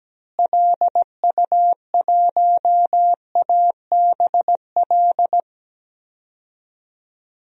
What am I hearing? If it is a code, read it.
Morse code, heard as a keyed tone, about 17 wpm, 700 Hz. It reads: LU1ABL
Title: DXCC-100 Call Signs - 6 Characters - Repeated 3x - 17wpm